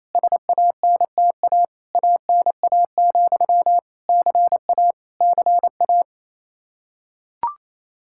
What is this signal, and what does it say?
Morse code, heard as a keyed tone, about 28 wpm, 700 Hz. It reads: SANTA ANA, CA CA
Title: Practice QSO - Exchange 1 - Beginning - 28wpm